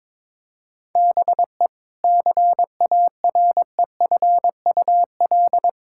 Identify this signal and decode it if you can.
Morse code, heard as a keyed tone, about 21 wpm, 700 Hz. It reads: BE CAREFUL
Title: Sentences from Top 1600 Words 22wpm